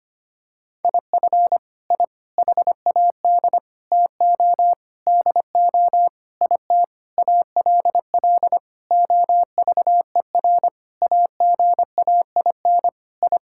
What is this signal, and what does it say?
Morse code, heard as a keyed tone, about 25 wpm, 700 Hz. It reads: IF I HAD TO DO IT ALL OVER AGAIN I
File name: practice / 25wpm / Sentences with 9-Word Phrases - 25wpm